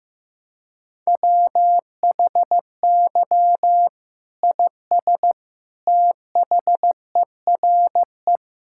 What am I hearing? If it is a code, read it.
Morse code, heard as a keyed tone, about 15 wpm, 700 Hz. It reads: WHY IS THERE
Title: Sentences from Top 1600 Words 15wpm